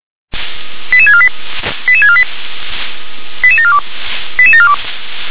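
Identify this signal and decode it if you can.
Five-tone selective call (ZVEI), heard as quick runs of five tones, tones of about 70 ms. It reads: 88648 88648 88742 88742
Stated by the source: ZVEI five-tone call received on an SDR.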